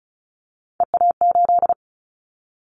Morse code, heard as a keyed tone, about 34 wpm, 700 Hz. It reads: EA8
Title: Callsign Prefixes for Top 50 DXCC - 35wpm